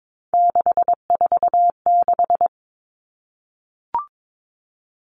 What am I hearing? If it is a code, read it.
Morse code, heard as a keyed tone, about 22 wpm, 700 Hz. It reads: 646
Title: Numbers - 3-Digit 22wpm